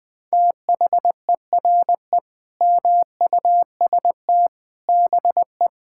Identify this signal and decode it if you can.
Morse code, heard as a keyed tone, about 21 wpm, 700 Hz. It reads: THERE MUST BE
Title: Sentences from Top 1000 Words - Review Part 6 - 20wpm